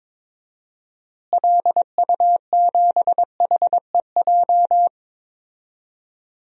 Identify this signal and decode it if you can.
Morse code, heard as a keyed tone, about 22 wpm, 700 Hz. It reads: LU7HEJ
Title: DXCC-100 Call Signs - 6 Characters - Repeated 3x - 22wpm